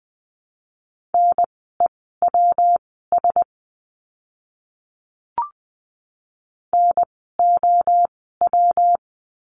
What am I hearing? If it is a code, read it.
Morse code, heard as a keyed tone, about 20 wpm, spaced as for 14 wpm, 700 Hz. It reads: NEWS NOW
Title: 115 Lesson 19f - Letter w - Words 20wpm